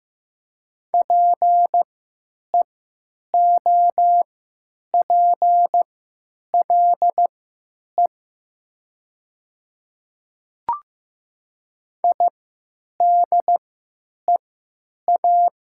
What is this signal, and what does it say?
Morse code, heard as a keyed tone, about 15 wpm, spaced as for 9 wpm, 700 Hz. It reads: PEOPLE IDEA
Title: 3x Character Spacing - Top 300 Words - 15wpm